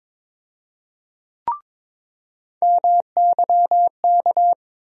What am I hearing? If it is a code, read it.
Morse code, heard as a keyed tone, about 22 wpm, 700 Hz. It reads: MYK
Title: USA Call Sign Postfixes - 3 Letters 22wpm